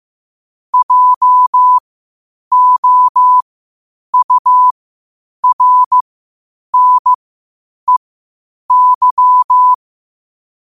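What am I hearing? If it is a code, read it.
Morse code, heard as a keyed tone, about 15 wpm, spaced as for 8 wpm, 1000 Hz. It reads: JOURNEY
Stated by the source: Morse code quotes at 8 WPM.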